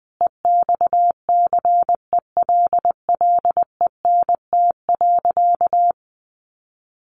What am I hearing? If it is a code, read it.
Morse code, heard as a keyed tone, about 19 wpm, 700 Hz. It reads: EXCELLENT.
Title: Sentences from Top 1000 Words 20wpm